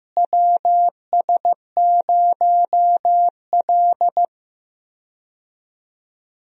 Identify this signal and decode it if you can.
Morse code, heard as a keyed tone, about 15 wpm, 700 Hz. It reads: WS0L
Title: USA Call Signs - 4 Characters - Repeated 3x - 15wpm